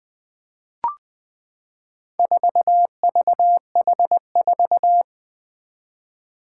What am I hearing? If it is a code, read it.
Morse code, heard as a keyed tone, about 20 wpm, 700 Hz. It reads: 4VH4
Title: Four Challenging Letter-Number Combinations 20wpm